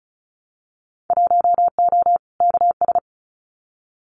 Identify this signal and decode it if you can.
Morse code, heard as a keyed tone, about 36 wpm, 700 Hz. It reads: 1O KS
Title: Winter Field Day Exchanges 35wpm